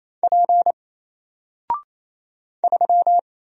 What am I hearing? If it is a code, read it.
Morse code, heard as a keyed tone, about 28 wpm, 700 Hz. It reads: P 3
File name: Single-Letter-Number-28wpm